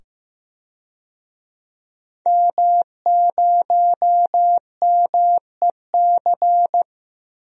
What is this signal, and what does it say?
Morse code, heard as a keyed tone, about 15 wpm, 700 Hz. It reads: M0MEC
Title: DXCC-100 Call Signs - 5 Characters - 15wpm